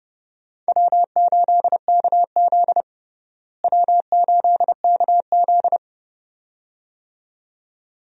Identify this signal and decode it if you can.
Morse code, heard as a keyed tone, about 30 wpm, 700 Hz. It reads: W8KZ W8KZ
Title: USA Call Signs - 4 Characters - Repeated 2x - 30wpm